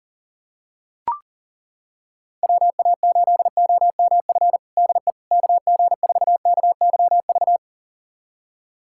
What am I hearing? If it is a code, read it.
Morse code, heard as a keyed tone, about 40 wpm, 700 Hz. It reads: WA8OMF DE KG4KYV
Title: USA Callsign DE Callsign - 6 Characters - 40wpm